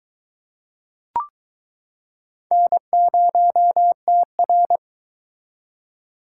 Morse code, heard as a keyed tone, about 23 wpm, 700 Hz. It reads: N0TR